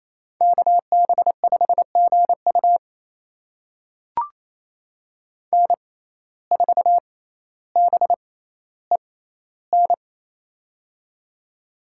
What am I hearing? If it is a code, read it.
Morse code, heard as a keyed tone, about 28 wpm, spaced as for 14 wpm, 700 Hz. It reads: KB5GU N4BEN
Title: USA Call Signs - 6x Character Spacing - 28wpm